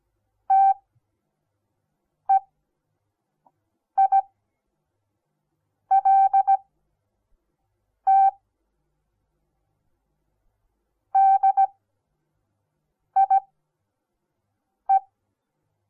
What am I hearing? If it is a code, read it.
Morse code, heard as a keyed tone, about 17 wpm, spaced as for 5 wpm, 780 Hz. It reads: TEILT DIE
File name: Onlinelehrgang / morsen / DJ4UF Morsen08